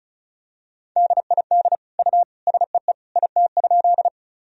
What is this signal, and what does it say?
Morse code, heard as a keyed tone, about 36 wpm, 700 Hz. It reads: DID U SEE IT?
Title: Sentences from Top 100 Words and Abbreviations 35wpm